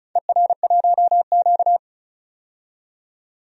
Morse code, heard as a keyed tone, about 36 wpm, 700 Hz. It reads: ER1Q
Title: DXCC-100 Call Signs - 4 Characters - 35wpm